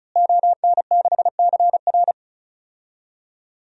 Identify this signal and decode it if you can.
Morse code, heard as a keyed tone, about 36 wpm, 700 Hz. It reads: ON6CR
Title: DXCC-100 Call Signs - 5 Characters - Repeated 2x - 35wpm